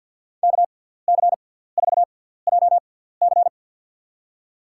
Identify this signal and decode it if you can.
Morse code, heard as a keyed tone, about 52 wpm, spaced as for 17 wpm, 700 Hz. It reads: KC4JC
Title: USA Call Signs - 5 Characters - 6x Character Spacing - 50wpm